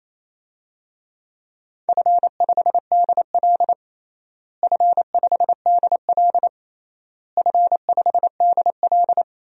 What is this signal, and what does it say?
Morse code, heard as a keyed tone, about 28 wpm, 700 Hz. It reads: F5DL F5DL F5DL
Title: DXCC-100 Call Signs - 4 Characters - Repeated 3x - 28wpm